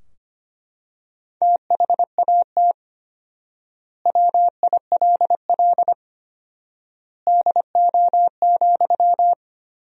Morse code, heard as a keyed tone, about 25 wpm, 700 Hz. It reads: THAT WILL DO,
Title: 4x Word-Spacing - Sentences from Top 100 Words - 25wpm